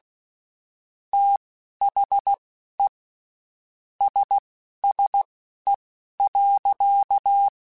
Morse code, heard as a keyed tone, about 16 wpm, spaced as for 11 wpm, 800 Hz. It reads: THE SSE.
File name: Spaced Characters / Spaced Characters-mp3s / Weather 4 14 sp @15